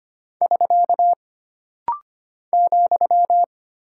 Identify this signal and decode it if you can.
Morse code, heard as a keyed tone, about 25 wpm, 700 Hz. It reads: <SK> ,
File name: Single Punctuation-Prosign - Rapid-Fire - 25wpm